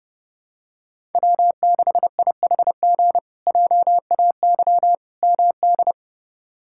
Morse code, heard as a keyed tone, about 29 wpm, 700 Hz. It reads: W6IHG JAY MD
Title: K1USN SST QSO Practice - 30wpm